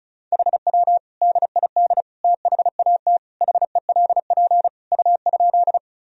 Morse code, heard as a keyed tone, about 36 wpm, 700 Hz. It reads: HW DID THAT HELP U?
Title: Sentences from Top 200 Words and Abbreviations 35wpm